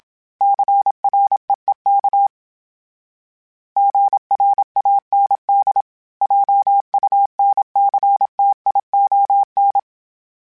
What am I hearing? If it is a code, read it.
Morse code, heard as a keyed tone, about 27 wpm, 800 Hz. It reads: CREEK GRAND JUNCTION
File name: fromSMs / FromSMs mp3s / Cities@25